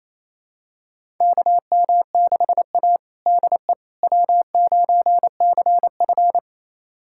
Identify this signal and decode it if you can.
Morse code, heard as a keyed tone, about 28 wpm, 700 Hz. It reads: KM6A DE W9CF
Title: USA Callsign DE Callsign - 4 Characters - 28wpm